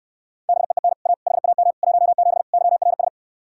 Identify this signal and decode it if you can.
Morse code, heard as a keyed tone, about 69 wpm, 700 Hz. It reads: BEEN A HAM 17 YRS